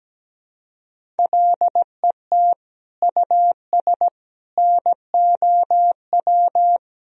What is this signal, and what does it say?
Morse code, heard as a keyed tone, about 17 wpm, 700 Hz. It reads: LET US NOW
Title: Sentences from Top 1000 Words - Review Part 1 - 17wpm